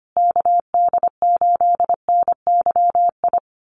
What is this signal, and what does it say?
Morse code, heard as a keyed tone, about 25 wpm, 700 Hz. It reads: KD8NYI
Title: USA Call Signs Repeated 3x 25wpm